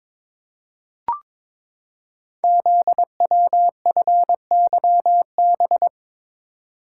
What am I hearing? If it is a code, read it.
Morse code, heard as a keyed tone, about 22 wpm, 700 Hz. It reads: ZWFYB
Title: Code Groups - 5 Letters - 22wpm